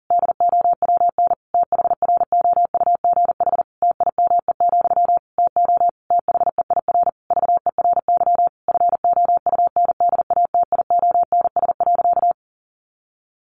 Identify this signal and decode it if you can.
Morse code, heard as a keyed tone, about 40 wpm, 700 Hz. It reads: DOWN THROUGH TIME, TO THEIR VERY FOUNDATIONS.
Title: A Sound Of Thunder 40wpm